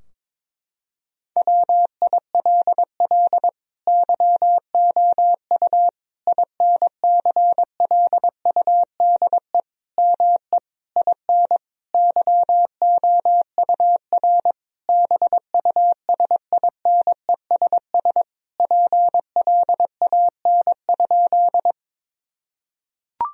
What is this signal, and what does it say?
Morse code, heard as a keyed tone, about 22 wpm, 700 Hz. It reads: WILL YOU INCLUDE ME IN YOUR BUSINESS PLAN?
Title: Sentences from Top 1000 Words - Review Part 3 - 22wpm